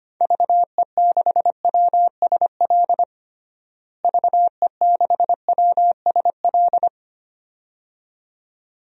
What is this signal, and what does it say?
Morse code, heard as a keyed tone, about 25 wpm, 700 Hz. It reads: VE6WSL VE6WSL
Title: DXCC-100 Call Signs - 6 Characters - Repeated 2x - 25wpm